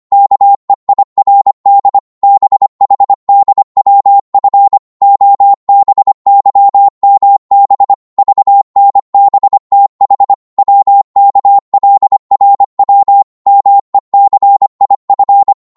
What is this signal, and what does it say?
Morse code, heard as a keyed tone, about 25 wpm, 800 Hz. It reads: KEIRD BHDWF OBYMB VNBTH WKLRW MECIF
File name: Morse / ltr-25wpm